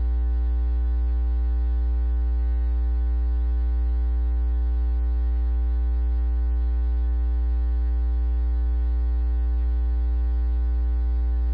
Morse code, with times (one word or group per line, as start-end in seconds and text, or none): none